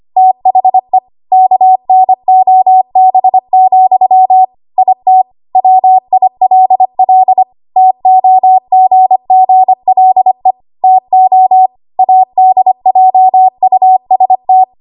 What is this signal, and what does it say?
Morse code, heard as a keyed tone, about 25 wpm, 750 Hz. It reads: THE KNOB, IT WILL TOGGLE TO ADJUST
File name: Archive / 25 WPM / 25WPM